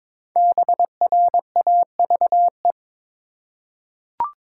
Morse code, heard as a keyed tone, about 22 wpm, 700 Hz. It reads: BRAVE
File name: 5 Letter Words - First 3 Letters Given - 22wpm